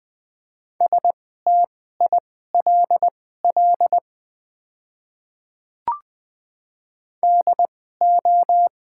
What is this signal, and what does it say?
Morse code, heard as a keyed tone, about 20 wpm, spaced as for 15 wpm, 700 Hz. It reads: STILL DO
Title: 2x Character Spacing - Top 300 Words - 20wpm